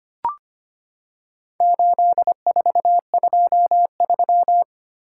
Morse code, heard as a keyed tone, about 25 wpm, 700 Hz. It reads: 8423
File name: practice / 25wpm / Four Letter-Number Combinations - 25wpm